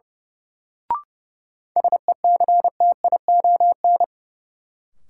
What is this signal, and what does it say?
Morse code, heard as a keyed tone, about 30 wpm, 700 Hz. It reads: SECTION
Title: Sentences from Top 1000 Words - Review Part 5 - New Word Drill - 30wpm